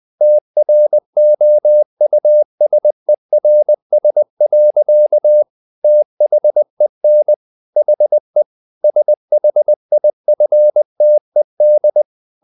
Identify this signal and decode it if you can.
Morse code, heard as a keyed tone, about 20 wpm, 600 Hz. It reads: TROUSERS. THEN HE SHIFTED
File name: the old man and the sea part 2 20wpm